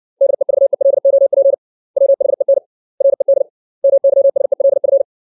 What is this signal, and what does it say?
Morse code, heard as a keyed tone, about 60 wpm, 550 Hz. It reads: DEVELOP WHEN RED MYSELF
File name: 632 words / 632 Words 60WPM